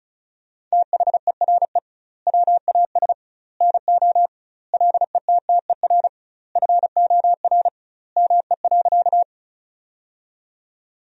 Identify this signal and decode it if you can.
Morse code, heard as a keyed tone, about 34 wpm, 700 Hz. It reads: THERE WAS NO LETTER FOR ME.
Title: 2x Word-Spacing - Sentences from Top 300 Words - 35wpm